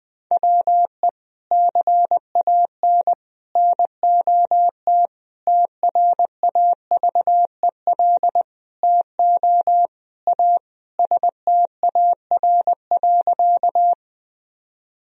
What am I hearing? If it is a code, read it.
Morse code, heard as a keyed tone, about 20 wpm, 700 Hz. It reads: WE CAN NOT TRAVEL TO A STAR.
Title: Sentences from Top 1000 Words - Review Part 2 - 20wpm